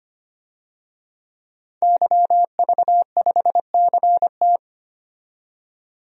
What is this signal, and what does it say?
Morse code, heard as a keyed tone, about 25 wpm, 700 Hz. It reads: YV5CT
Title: DXCC-100 Call Signs - 5 Characters - 25wpm